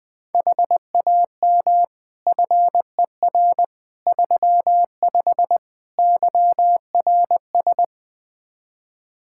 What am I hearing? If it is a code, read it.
Morse code, heard as a keyed tone, about 20 wpm, 700 Hz. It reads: HAM FER 35 YRS
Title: Practice QSO - Exchange 3 - More Info - 20wpm